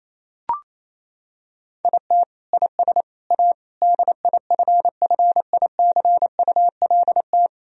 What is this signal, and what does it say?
Morse code, heard as a keyed tone, about 28 wpm, 700 Hz. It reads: IT IS A DIFFICULT
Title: Sentences from Top 1000 Words - Review Part 3 - 28wpm